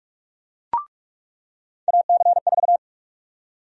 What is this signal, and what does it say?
Morse code, heard as a keyed tone, about 46 wpm, 700 Hz. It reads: AC4
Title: USA Call Sign Prefixes - 45wpm